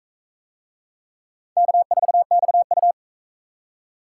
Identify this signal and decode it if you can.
Morse code, heard as a keyed tone, about 43 wpm, 700 Hz. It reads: K4XU